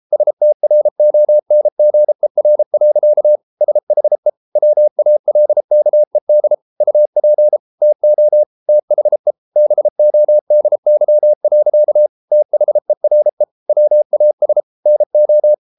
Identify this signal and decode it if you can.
Morse code, heard as a keyed tone, about 33 wpm, 600 Hz. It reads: STRONGER. SHE WALKED UP TO THE BODY. THERE WAS NO